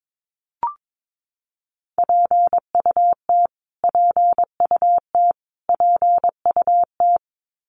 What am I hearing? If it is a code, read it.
Morse code, heard as a keyed tone, about 22 wpm, 700 Hz. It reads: PUT PUT PUT
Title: Top 500 Words Repeated 22wpm